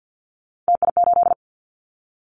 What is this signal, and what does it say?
Morse code, heard as a keyed tone, about 50 wpm, 700 Hz. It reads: TI8
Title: Callsign Prefixes for Top 100 DXCC - 50wpm